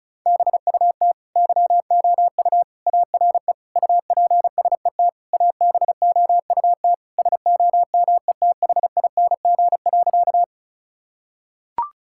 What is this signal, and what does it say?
Morse code, heard as a keyed tone, about 36 wpm, 700 Hz. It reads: BUT YOU ARE UPSET ABOUT SOMETHING.
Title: Sentences from Top 1000 Words - Review Part 5 - 35wpm